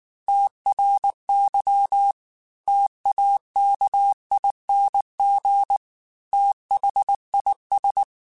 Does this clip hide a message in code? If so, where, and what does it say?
Yes: Morse code from 0.3 to 8.0 s, heard as a keyed tone, about 19 wpm, 800 Hz. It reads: TRY TAKING THIS